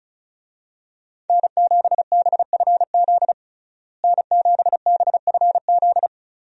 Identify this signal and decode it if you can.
Morse code, heard as a keyed tone, about 36 wpm, 700 Hz. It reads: N7BFZ N7BFZ
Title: USA Call Signs - 5 Characters - Repeated 2x - 35wpm